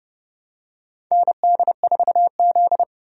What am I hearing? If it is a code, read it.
Morse code, heard as a keyed tone, about 32 wpm, 700 Hz. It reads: ND4Z
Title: USA Call Signs - 4 Characters - 30wpm